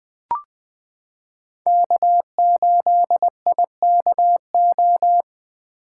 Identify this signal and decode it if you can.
Morse code, heard as a keyed tone, about 20 wpm, 700 Hz. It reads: K8IKO
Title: 2023 POTA Hunter Call Signs - 20wpm